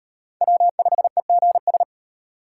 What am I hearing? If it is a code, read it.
Morse code, heard as a keyed tone, about 38 wpm, 700 Hz. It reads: W5EGS